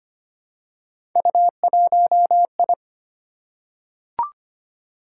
Morse code, heard as a keyed tone, about 25 wpm, 700 Hz. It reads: U1I
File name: Three Letter-Number Combinations - 25wpm